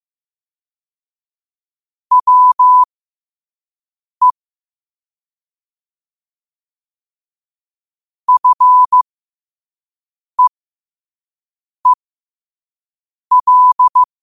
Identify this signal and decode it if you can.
Morse code, heard as a keyed tone, about 15 wpm, spaced as for 5 wpm, 1000 Hz. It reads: WE FEEL
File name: qotdfiles / 5wpm